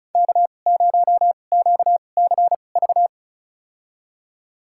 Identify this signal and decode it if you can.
Morse code, heard as a keyed tone, about 36 wpm, spaced as for 25 wpm, 700 Hz. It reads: K0QCV
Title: USA Call Signs - 5 Characters - 2x Character Spacing - 35wpm